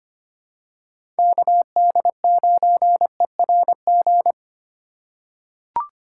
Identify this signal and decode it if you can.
Morse code, heard as a keyed tone, about 25 wpm, 700 Hz. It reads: KD9ERG
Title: USA Call Signs - 6 Characters - 2x Character Spacing - 25wpm